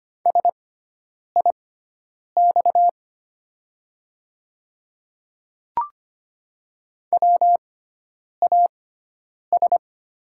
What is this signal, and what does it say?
Morse code, heard as a keyed tone, about 25 wpm, spaced as for 9 wpm, 700 Hz. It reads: SIX WAS
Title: 6x Character Spacing - Top 500 Words - 25wpm